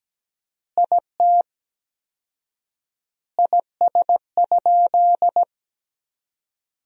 Morse code, heard as a keyed tone, about 17 wpm, 700 Hz. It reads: IT IS?
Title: 4x Word-Spacing - Sentences from Top 100 Words - 17wpm